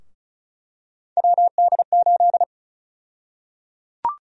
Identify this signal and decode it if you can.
Morse code, heard as a keyed tone, about 36 wpm, 700 Hz. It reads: WD8
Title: USA Call Sign Prefixes - Repeated 2x - 35wpm